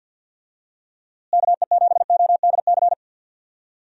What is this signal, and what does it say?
Morse code, heard as a keyed tone, about 52 wpm, 700 Hz. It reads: KE7ODC
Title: USA Call Signs - 6 Characters - Repeated 3x - 50wpm